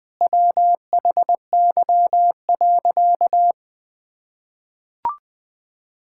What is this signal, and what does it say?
Morse code, heard as a keyed tone, about 21 wpm, 700 Hz. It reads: WHY.
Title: Sentences with 3-Word Phrases - 20wpm